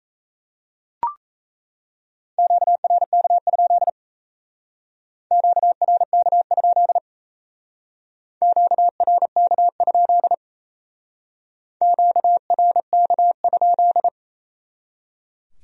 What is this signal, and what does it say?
Morse code, heard as a keyed tone, about 36 wpm, 700 Hz. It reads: QRK? QRK? QRK? QRK?